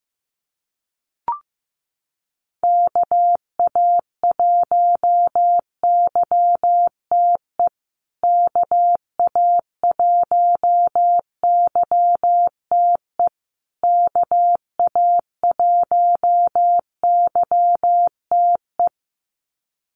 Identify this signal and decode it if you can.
Morse code, heard as a keyed tone, about 15 wpm, 700 Hz. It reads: KA1YTE KA1YTE KA1YTE
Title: USA Call Signs Repeated 3x 15wpm